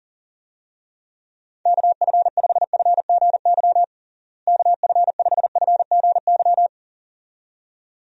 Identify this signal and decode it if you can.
Morse code, heard as a keyed tone, about 40 wpm, 700 Hz. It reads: KF5FGY KF5FGY
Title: USA Call Signs - 6 Characters - Repeated 2x - 40wpm